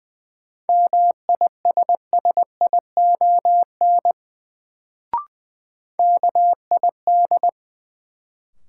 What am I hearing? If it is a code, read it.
Morse code, heard as a keyed tone, about 20 wpm, 700 Hz. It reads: MISSION KID
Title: Sentences from Top 1600 Words - New Word Drill Alt - 20wpm